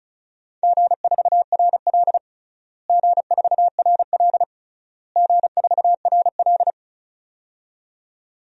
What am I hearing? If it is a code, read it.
Morse code, heard as a keyed tone, about 34 wpm, 700 Hz. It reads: G4RL G4RL G4RL
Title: DXCC-100 Call Signs - 4 Characters - Repeated 3x - 35wpm